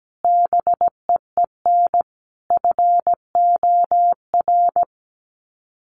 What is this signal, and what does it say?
Morse code, heard as a keyed tone, about 17 wpm, 700 Hz. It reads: BEEN FOR
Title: Sets of 5 Words Encore2 17wpm